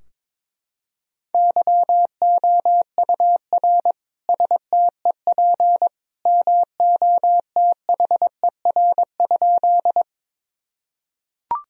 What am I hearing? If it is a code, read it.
Morse code, heard as a keyed tone, about 22 wpm, 700 Hz. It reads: YOUR STEP MOTHER?
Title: Sentences from Top 1000 Words - Review Part 2 - 22wpm